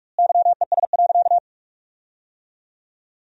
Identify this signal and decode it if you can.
Morse code, heard as a keyed tone, about 43 wpm, 700 Hz. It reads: YES.
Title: 2x Word-Spacing - Sentences from Top 500 Words - 45wpm